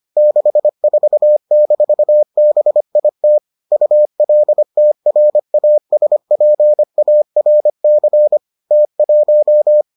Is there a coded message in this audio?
Yes: Morse code from 0.2 to 9.8 s, heard as a keyed tone, about 25 wpm, 600 Hz. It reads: 64-BIT ULTRASPARC T1